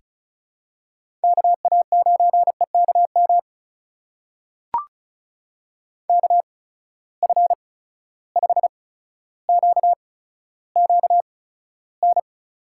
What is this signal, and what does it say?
Morse code, heard as a keyed tone, about 36 wpm, spaced as for 15 wpm, 700 Hz. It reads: KA9EKM KF5QQN
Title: USA Call Signs - 6 Characters - 8x Character Spacing - 35wpm